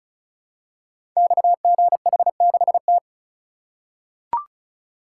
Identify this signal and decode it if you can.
Morse code, heard as a keyed tone, about 36 wpm, 690 Hz. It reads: XGH6T
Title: Code Groups - 5 Letter:Numbers - 35wpm